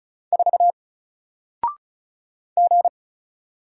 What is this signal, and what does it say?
Morse code, heard as a keyed tone, about 35 wpm, 700 Hz. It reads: 4 G
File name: Single-Letter-Number-35wpm